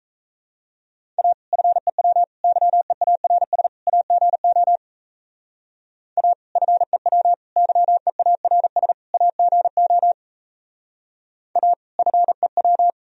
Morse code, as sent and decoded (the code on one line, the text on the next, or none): .- / ..-. .-- / -.-- . .- .-. ... / .- --. --- / .- / ..-. .-- / -.-- . .- .-. ... / .- --. --- / .- / ..-. .--
A FEW YEARS AGO A FEW YEARS AGO A FEW